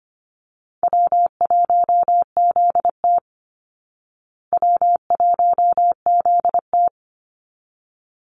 Morse code, heard as a keyed tone, about 25 wpm, 700 Hz. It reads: W1ZT W1ZT
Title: Easy Call Signs Repeated 2x - 25wpm